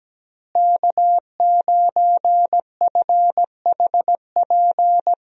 Morse code, heard as a keyed tone, about 17 wpm, 700 Hz. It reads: K9FHP
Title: USA Call Signs - 5 Characters - Repeated 3x - 17wpm